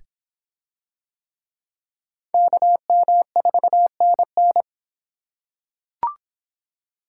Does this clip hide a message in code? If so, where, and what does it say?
Yes: Morse code from 2.3 to 4.6 s, heard as a keyed tone, about 25 wpm, 700 Hz. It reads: KM4NN